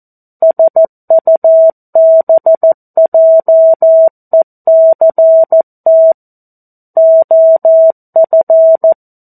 Morse code, heard as a keyed tone, about 14 wpm, 650 Hz. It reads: SUBJECT OF
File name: 02-Chapter-02 14 WPM